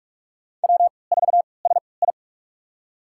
Morse code, heard as a keyed tone, about 44 wpm, spaced as for 25 wpm, 690 Hz. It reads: W4SI
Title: USA Call Signs - 4 Characters - 3x Character Spacing - 45wpm